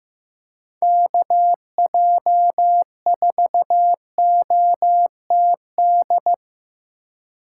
Morse code, heard as a keyed tone, about 18 wpm, 700 Hz. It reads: KJ4OTD
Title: Speed Racing - USA Call Signs 15wpm